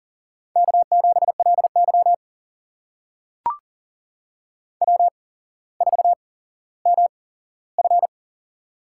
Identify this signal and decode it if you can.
Morse code, heard as a keyed tone, about 42 wpm, spaced as for 17 wpm, 700 Hz. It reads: K7LY W4MF
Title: USA Call Signs - 4 Characters - 8x Character Spacing - 40wpm